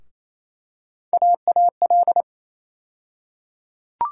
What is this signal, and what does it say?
Morse code, heard as a keyed tone, about 28 wpm, 700 Hz. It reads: AAL
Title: USA Call Sign Postfixes - 3 Letters 28wpm